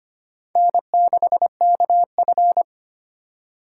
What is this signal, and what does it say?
Morse code, heard as a keyed tone, about 31 wpm, 700 Hz. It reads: N6KF
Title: Speed Racing - USA 1x2 Call Signs 25wpm